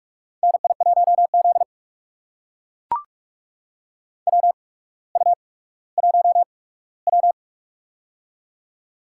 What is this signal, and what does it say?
Morse code, heard as a keyed tone, about 44 wpm, spaced as for 19 wpm, 700 Hz. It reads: NI1Z WU1W
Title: USA Call Signs - 4 Characters - 8x Character Spacing - 45wpm